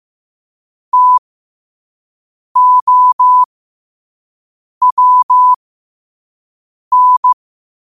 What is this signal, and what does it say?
Morse code, heard as a keyed tone, about 15 wpm, spaced as for 5 wpm, 1000 Hz. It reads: TOWN
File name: qotdfiles / 5wpm